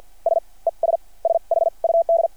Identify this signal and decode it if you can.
Morse code, heard as a keyed tone, about 52 wpm, spaced as for 40 wpm, 660 Hz. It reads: S ES SHUD
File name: archived / CRSnet